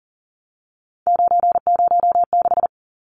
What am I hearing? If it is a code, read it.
Morse code, heard as a keyed tone, about 40 wpm, 700 Hz. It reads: 906
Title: Numbers - 3-Digit 40wpm